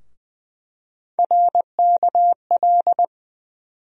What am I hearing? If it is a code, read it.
Morse code, heard as a keyed tone, about 20 wpm, 700 Hz. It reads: RKL
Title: USA Call Sign Postfixes - 3 Letters 20wpm